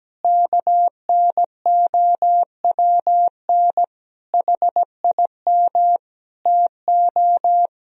Morse code, heard as a keyed tone, about 17 wpm, 700 Hz. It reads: KNOWN HIM TO
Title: Sentences from Top 1000 Words - Review Part 3 - 17wpm